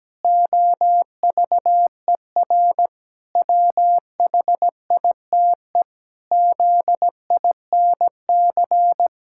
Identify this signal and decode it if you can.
Morse code, heard as a keyed tone, about 17 wpm, 700 Hz. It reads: OVER WHITE ZINC
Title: Pangram Challenge v2 - 17wpm